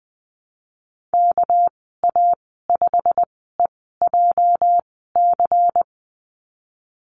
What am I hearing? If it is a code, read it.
Morse code, heard as a keyed tone, about 20 wpm, spaced as for 15 wpm, 700 Hz. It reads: KA5EJC
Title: 213 Lesson 34g - Number 8 - Call Signs 20wpm